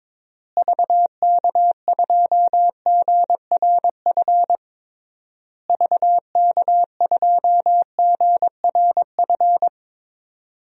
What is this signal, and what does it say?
Morse code, heard as a keyed tone, about 22 wpm, 700 Hz. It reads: VK2GRF VK2GRF
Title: DXCC-100 Call Signs - 6 Characters - Repeated 3x - 22wpm